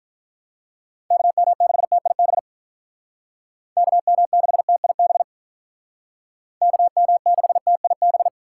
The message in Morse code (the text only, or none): KM6TIB KM6TIB KM6TIB